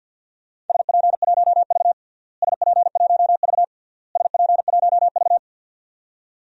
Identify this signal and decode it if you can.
Morse code, heard as a keyed tone, about 50 wpm, 700 Hz. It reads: SP1V SP1V SP1V
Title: DXCC-100 Call Signs - 4 Characters - Repeated 3x - 50wpm